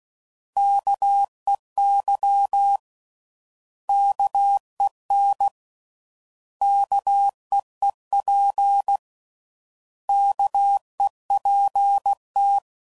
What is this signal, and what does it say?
Morse code, heard as a keyed tone, about 16 wpm, 800 Hz. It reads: KEY KEN KEEP KEPT